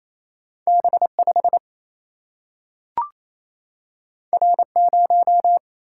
Morse code, heard as a keyed tone, about 28 wpm, 700 Hz. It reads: B5 R0
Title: Single-Letter Single-Number 28wpm